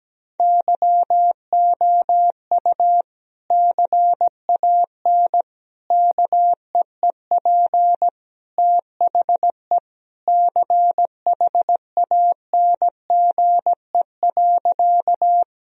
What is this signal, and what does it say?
Morse code, heard as a keyed tone, about 17 wpm, 700 Hz. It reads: YOU CAN KEEP THE CHANGE.
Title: Sentences from Top 1000 Words - Review Part 1 - 17wpm